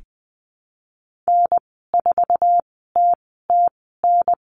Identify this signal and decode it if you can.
Morse code, heard as a keyed tone, about 20 wpm, spaced as for 14 wpm, 700 Hz. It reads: N4TTN
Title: 082 Lesson 14g - Number 2 - Call Signs 20wpm